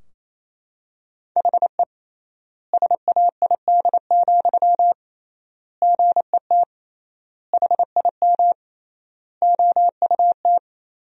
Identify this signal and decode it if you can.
Morse code, heard as a keyed tone, about 28 wpm, 700 Hz. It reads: HE SAID, GET HIM OUT
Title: 3x Word-Spacing - Sentences from Top 400 Words - 28wpm